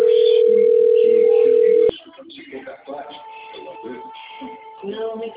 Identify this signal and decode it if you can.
Ringback tone (North America), 440 + 480 Hz, with speech.
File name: out- - - - - .89298